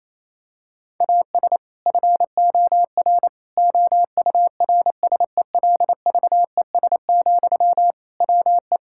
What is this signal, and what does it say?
Morse code, heard as a keyed tone, about 28 wpm, 700 Hz. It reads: AS FOR OURSELVES, WE
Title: Sentences from Top 1600 Words 28wpm